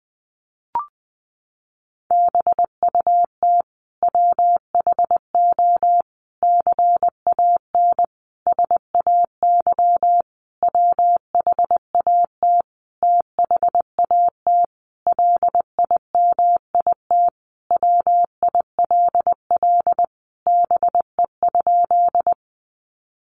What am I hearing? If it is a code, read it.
Morse code, heard as a keyed tone, about 20 wpm, 700 Hz. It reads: BUT WHO CAN SAY WHAT THAT LIMIT WILL BE?
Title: Sentences from Top 1500 Words 20wpm